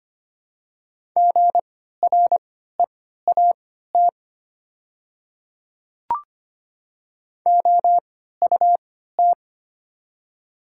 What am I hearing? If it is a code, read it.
Morse code, heard as a keyed tone, about 25 wpm, spaced as for 14 wpm, 700 Hz. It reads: GREAT OUT
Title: 3x Character Spacing - Top 200 Words - 25wpm